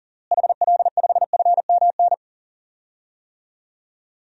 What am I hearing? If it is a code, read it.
Morse code, heard as a keyed tone, about 40 wpm, 700 Hz. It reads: HL5FMN
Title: DXCC-100 Call Signs - 6 Characters - Repeated 2x - 40wpm